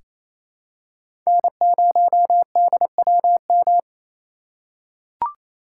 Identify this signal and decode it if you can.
Morse code, heard as a keyed tone, about 28 wpm, 700 Hz. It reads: N0DWM
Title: USA Call Signs - 5 Characters - 6x Character Spacing - 28wpm